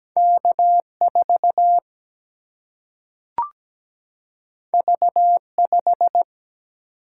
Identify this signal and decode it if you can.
Morse code, heard as a keyed tone, about 17 wpm, 700 Hz. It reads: K 4 V 5
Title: Single-Letter Single-Number 17wpm